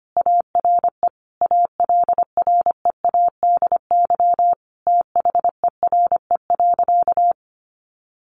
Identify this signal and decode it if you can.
Morse code, heard as a keyed tone, about 25 wpm, 700 Hz. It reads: ARE ALREADY THERE.
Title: Sentences from Top 700 Words 25wpm